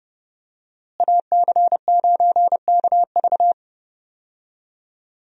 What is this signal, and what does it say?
Morse code, heard as a keyed tone, about 30 wpm, 700 Hz. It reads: AC9KV
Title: USA Call Signs - 5 Characters - Repeated 2x - 30wpm